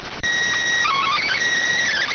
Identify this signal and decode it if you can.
SSTV (slow-scan television), PD 290.